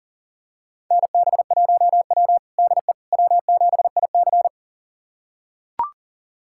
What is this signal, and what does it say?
Morse code, heard as a keyed tone, about 40 wpm, 700 Hz. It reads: NB1W DE W7IC
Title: USA Callsign DE Callsign - 4 Characters - 40wpm